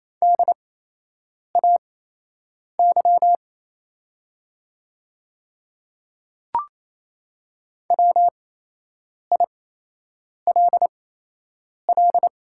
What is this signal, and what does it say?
Morse code, heard as a keyed tone, about 27 wpm, spaced as for 8 wpm, 700 Hz. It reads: DAY WILL
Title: 8x Character Spacing - Top 100 Words - 28wpm